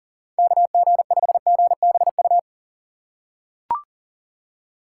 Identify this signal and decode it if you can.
Morse code, heard as a keyed tone, about 40 wpm, 700 Hz. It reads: KG5GBU
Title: USA Call Signs - 6 Characters - 6x Character Spacing - 40wpm